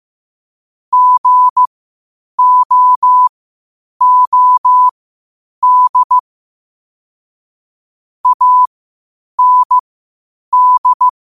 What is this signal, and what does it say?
Morse code, heard as a keyed tone, about 15 wpm, spaced as for 8 wpm, 1000 Hz. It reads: GOOD AND
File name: qotdfiles / 8wpm